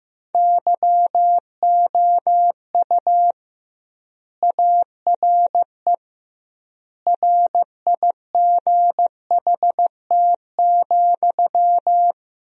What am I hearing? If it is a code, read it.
Morse code, heard as a keyed tone, about 15 wpm, 700 Hz. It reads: YOU ARE RIGHT,
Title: 2x Word-Spacing - Sentences from Top 300 Words - 15wpm